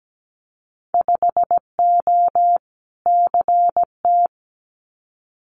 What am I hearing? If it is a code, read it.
Morse code, heard as a keyed tone, about 17 wpm, 700 Hz. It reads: 5O CT
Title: Winter Field Day Exchanges Encore 17wpm